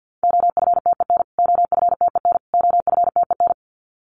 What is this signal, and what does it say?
Morse code, heard as a keyed tone, about 50 wpm, 700 Hz. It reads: OFTEN OFTEN OFTEN
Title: Top 300 Words Repeated 50wpm